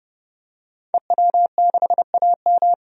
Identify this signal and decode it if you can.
Morse code, heard as a keyed tone, about 30 wpm, 700 Hz. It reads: EW6AM
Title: DXCC-100 Call Signs - 5 Characters - 30wpm